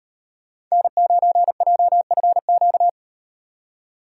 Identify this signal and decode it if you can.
Morse code, heard as a keyed tone, about 38 wpm, 700 Hz. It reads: N9JFQ